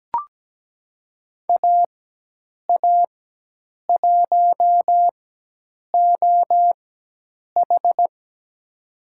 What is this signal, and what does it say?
Morse code, heard as a keyed tone, about 17 wpm, spaced as for 8 wpm, 700 Hz. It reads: AA1OH
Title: USA Call Signs - 4x Character Spacing - 17wpm